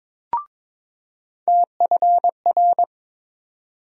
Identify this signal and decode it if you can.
Morse code, heard as a keyed tone, about 22 wpm, 700 Hz. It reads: TFR
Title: USA Call Sign Postfixes - 3 Letters 22wpm